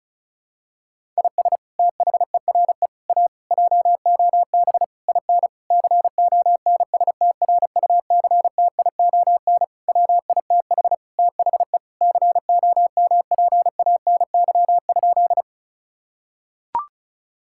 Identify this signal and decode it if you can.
Morse code, heard as a keyed tone, about 36 wpm, 700 Hz. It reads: IS THERE A JOB IN CONSTRUCTION WITH THE COMPANY?
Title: Sentences from Top 1600 Words 35wpm